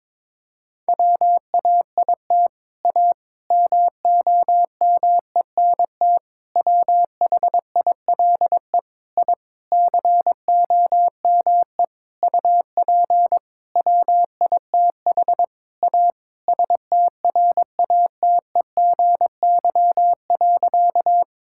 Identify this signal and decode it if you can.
Morse code, heard as a keyed tone, about 22 wpm, 700 Hz. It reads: WAIT A MOMENT WHILE I COME UP WITH A STRATEGY.
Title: Sentences from Top 1000 Words - Review Part 5 - 22wpm